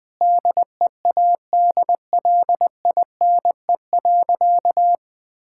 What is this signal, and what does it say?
Morse code, heard as a keyed tone, about 20 wpm, 700 Hz. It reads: DEADLINE.
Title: Sentences with 8-Word Phrases - 20wpm